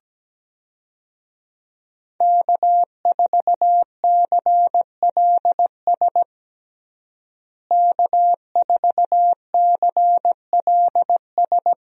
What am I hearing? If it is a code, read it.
Morse code, heard as a keyed tone, about 17 wpm, 700 Hz. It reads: K4CLS K4CLS
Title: USA Call Signs - 5 Characters - Repeated 2x - 17wpm